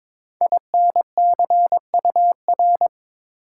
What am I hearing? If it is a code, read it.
Morse code, heard as a keyed tone, about 22 wpm, 700 Hz. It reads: INCUR
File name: 5 Letter Words - First 3 Letters Given - 22wpm